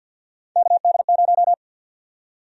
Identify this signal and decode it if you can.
Morse code, heard as a keyed tone, about 52 wpm, 700 Hz. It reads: KD0
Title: USA Call Sign Prefixes - 50wpm